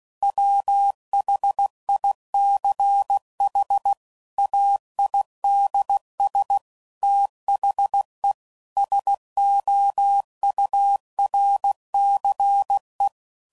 Morse code, heard as a keyed tone, about 16 wpm, 800 Hz. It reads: WHICH AIDS THE SOURCE
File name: CW Code Groups @15WPM / Technical Subjects / XC